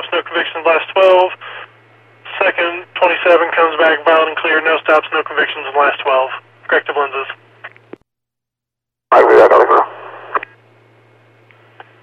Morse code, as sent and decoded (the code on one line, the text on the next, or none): none